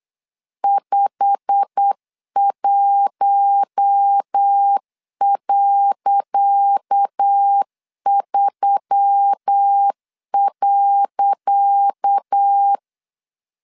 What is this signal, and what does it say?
Morse code, heard as a keyed tone, about 8 wpm, 800 Hz. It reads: 51.3.
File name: LEZIONI / 10 SIGNS / 10C DOT 8wpm 3649